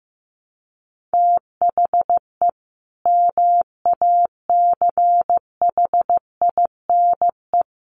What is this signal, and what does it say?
Morse code, heard as a keyed tone, about 15 wpm, 700 Hz. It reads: THE MACHINE
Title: A Sound Of Thunder 15wpm